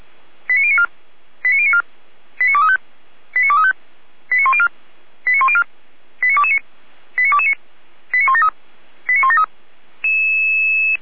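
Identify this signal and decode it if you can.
Five-tone selective call (ZVEI), heard as quick runs of five tones, tones of about 70 ms. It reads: 89094 89094 89235 89235 89104 89104 89229 89229 89173 89173
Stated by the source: ZVEI five-tone selective call received on SDR.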